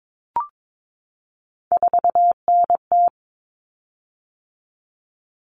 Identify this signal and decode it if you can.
Morse code, heard as a keyed tone, about 22 wpm, 700 Hz. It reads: 4NT
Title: Common Cut Numbers - 3-Digit 22wpm